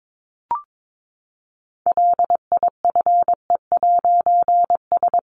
Morse code, heard as a keyed tone, about 22 wpm, 700 Hz. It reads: LIFE'S
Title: A Sound Of Thunder 22wpm